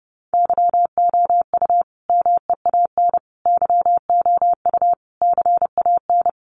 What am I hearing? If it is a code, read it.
Morse code, heard as a keyed tone, about 30 wpm, 700 Hz. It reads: YOU MEAN YOU CAN
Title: Sentences from Top 1400 Words 30wpm